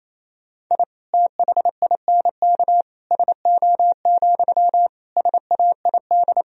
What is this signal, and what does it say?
Morse code, heard as a keyed tone, about 28 wpm, 700 Hz. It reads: I THINK SO, SAID
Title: Sentences from Top 1000 Words - Review Part 3 - 28wpm